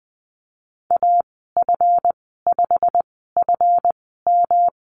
Morse code, heard as a keyed tone, about 20 wpm, spaced as for 15 wpm, 700 Hz. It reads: AF5FM
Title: 193 Lesson 31g - Letter v - Call Signs 20wpm